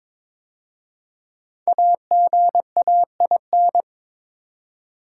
Morse code, heard as a keyed tone, about 22 wpm, 700 Hz. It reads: AGAIN
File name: 5 Letter Words - First 3 Letters Given - 22wpm